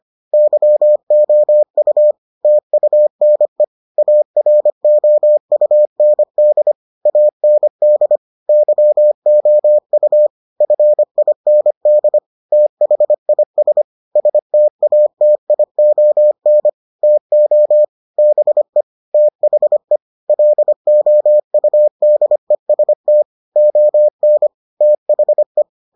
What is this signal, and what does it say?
Morse code, heard as a keyed tone, about 25 wpm, 600 Hz. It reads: YOU TUNE AROUND AND YOU FIND THIS STATION TO BE THE LOUDEST ON THE